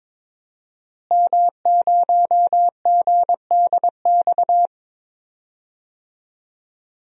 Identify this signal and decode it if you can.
Morse code, heard as a keyed tone, about 22 wpm, 700 Hz. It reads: M0GDX
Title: DXCC-100 Call Signs - 5 Characters - Repeated 2x - 22wpm